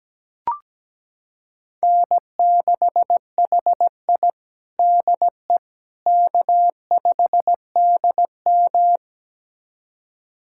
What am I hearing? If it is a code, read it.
Morse code, heard as a keyed tone, about 17 wpm, 700 Hz. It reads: N6HI DE K5DM
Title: USA Callsign DE Callsign - 4 Characters - 17wpm